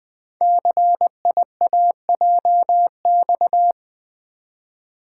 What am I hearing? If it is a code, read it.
Morse code, heard as a keyed tone, about 20 wpm, 700 Hz. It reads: CIAJX
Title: Code Groups - 5 Letters - 20wpm